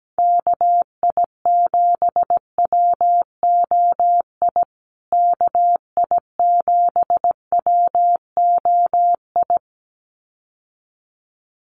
Morse code, heard as a keyed tone, about 17 wpm, 700 Hz. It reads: KI7WOI KI7WOI
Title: USA Call Signs Repeated 2x 17wpm